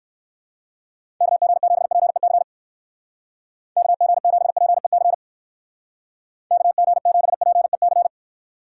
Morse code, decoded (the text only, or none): KG7PEC KG7PEC KG7PEC